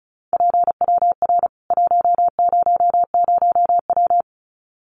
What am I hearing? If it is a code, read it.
Morse code, heard as a keyed tone, about 34 wpm, 700 Hz. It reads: PWR 100W
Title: Common QSO Phrases 35wpm